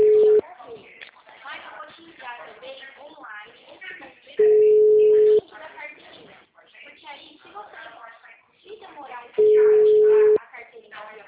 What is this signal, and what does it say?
Ringback tone (Europe, 425 Hz), with speech.
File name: out- -2007- - - .66347